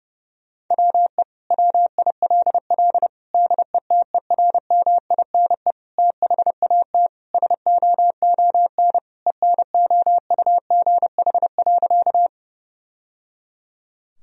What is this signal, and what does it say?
Morse code, heard as a keyed tone, about 32 wpm, 700 Hz. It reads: WE WILL DETERMINE THAT SOON ENOUGH.
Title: Sentences from Top 1000 Words - Review Part 4 - 30wpm